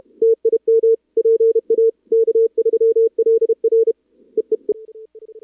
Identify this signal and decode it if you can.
Morse code, heard as a keyed tone, about 31 wpm, 440 Hz. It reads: TIM PA K3LR EEE
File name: CWSprint / Feb2022 / k5tr-feb-na-sprint-cw-k5ot -0000